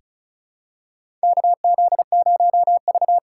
Morse code, heard as a keyed tone, about 34 wpm, 700 Hz. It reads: KZ0V
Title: USA Call Signs - 4 Characters - 3x Character Spacing - 35wpm